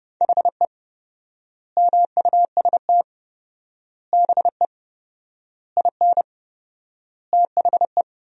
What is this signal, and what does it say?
Morse code, heard as a keyed tone, about 30 wpm, 700 Hz. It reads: HE MUST BE IN THE
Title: 4x Word-Spacing - Sentences from Top 500 Words - 30wpm